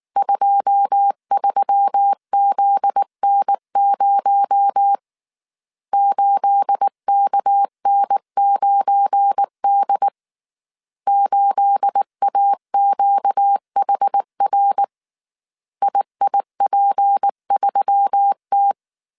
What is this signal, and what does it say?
Morse code, heard as a keyed tone, about 19 wpm, 800 Hz. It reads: 23ZN0 8KN9D 8AQHR IIP3T
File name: CW / DET / 1 9 1500C S A 18wpm 2459